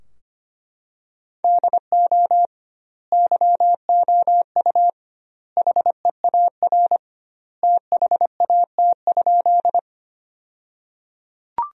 Morse code, heard as a keyed tone, about 25 wpm, 700 Hz. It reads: DO YOU HEAR THAT?
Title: 2x Word-Spacing - Sentences from Top 300 Words - 25wpm